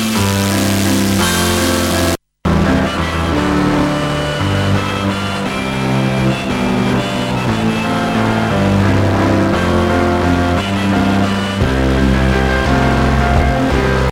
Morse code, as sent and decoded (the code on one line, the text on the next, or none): none